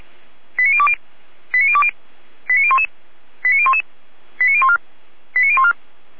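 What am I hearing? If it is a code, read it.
Five-tone selective call (ZVEI), heard as quick runs of five tones, tones of about 70 ms. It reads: 89020 89020 89011 89011 89014 89014
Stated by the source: ZVEI five-tone call 89001 received by SDR, from the server of Feuerwehr Orlamünde (fire brigade).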